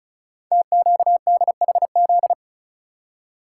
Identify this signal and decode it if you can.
Morse code, heard as a keyed tone, about 36 wpm, 700 Hz. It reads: TQDHZ
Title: Code Groups - 5 Letter:Numbers - 35wpm